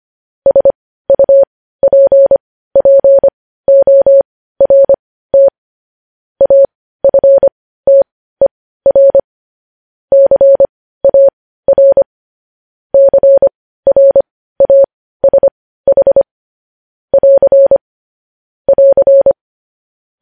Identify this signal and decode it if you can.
Morse code, heard as a keyed tone, about 25 wpm, spaced as for 15 wpm, 550 Hz. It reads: SUPPORT AFTER CAR CRASH <AR> <AR>
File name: podcast / News headlines-1x-25x15